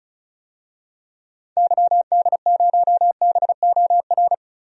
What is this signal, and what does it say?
Morse code, heard as a keyed tone, about 34 wpm, 700 Hz. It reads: YD0BOR
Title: DXCC-100 Call Signs - 6 Characters - 35wpm